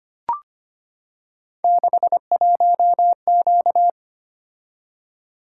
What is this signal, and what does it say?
Morse code, heard as a keyed tone, about 25 wpm, 700 Hz. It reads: 61Q
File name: Three Letter-Number Combinations - 25wpm